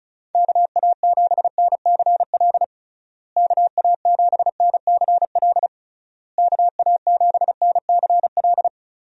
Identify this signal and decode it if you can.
Morse code, heard as a keyed tone, about 36 wpm, 700 Hz. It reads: KA7NCL KA7NCL KA7NCL
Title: USA Call Signs - 6 Characters - Repeated 3x - 35wpm